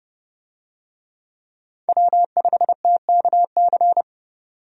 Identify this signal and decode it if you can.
Morse code, heard as a keyed tone, about 30 wpm, 700 Hz. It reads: W5TKC
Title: USA Call Signs - 5 Characters - 30wpm